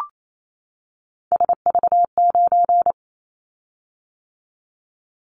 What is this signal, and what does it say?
Morse code, heard as a keyed tone, about 28 wpm, 700 Hz. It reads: SV9
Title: Callsign Prefixes for Top 100 DXCC - 28wpm